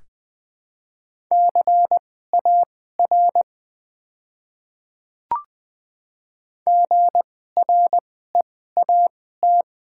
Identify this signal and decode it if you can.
Morse code, heard as a keyed tone, about 20 wpm, spaced as for 14 wpm, 700 Hz. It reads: CAR GREAT
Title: 2x Character Spacing - Top 300 Words - 20wpm